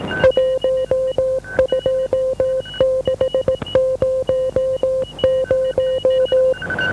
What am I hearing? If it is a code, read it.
Morse code, heard as a keyed tone, about 18 wpm, 520 Hz. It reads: 12600